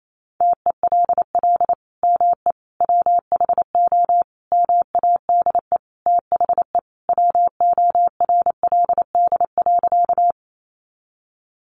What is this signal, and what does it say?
Morse code, heard as a keyed tone, about 28 wpm, 700 Hz. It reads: TELL ME WHO MADE THE WORLD.
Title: Sentences from Top 200 Words 28wpm